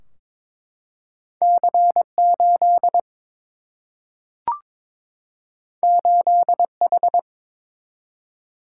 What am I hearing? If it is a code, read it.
Morse code, heard as a keyed tone, about 21 wpm, 700 Hz. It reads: C8 8H
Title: Single-Letter Single-Number 22wpm